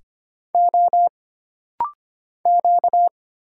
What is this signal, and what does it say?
Morse code, heard as a keyed tone, about 25 wpm, 700 Hz. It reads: O Q
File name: Single Character (Letter:Number:Punctuation:Prosign) - Rapid-Fire - 25wpm